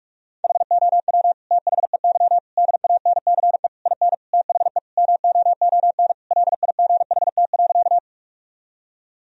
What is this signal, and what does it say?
Morse code, heard as a keyed tone, about 42 wpm, 700 Hz. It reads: HOW THEY DANCE IN THE MOON LIGHT.
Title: Sentences from Top 1000 Words - Review Part 4 - 45wpm